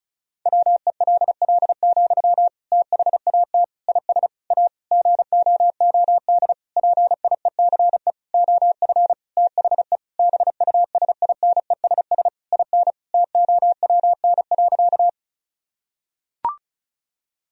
Morse code, heard as a keyed tone, about 36 wpm, 700 Hz. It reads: WELL, THAT IS A GOOD PIECE OF THE BUSINESS IN TOWN.
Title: Sentences from Top 1000 Words - Review Part 2 - 35wpm